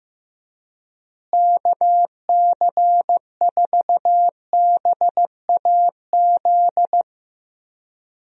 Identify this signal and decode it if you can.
Morse code, heard as a keyed tone, about 15 wpm, 700 Hz. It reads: KC4BAZ
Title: USA Call Signs - 6 Characters - Repeated 3x - 15wpm